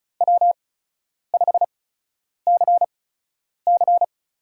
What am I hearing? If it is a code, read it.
Morse code, heard as a keyed tone, about 34 wpm, spaced as for 10 wpm, 700 Hz. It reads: W5CC
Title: USA Call Signs - 4 Characters - 8x Character Spacing - 35wpm